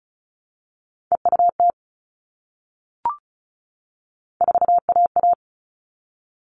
Morse code, heard as a keyed tone, about 34 wpm, 700 Hz. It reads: EUT 4AA
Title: All Cut Numbers - 3-Digit 35wpm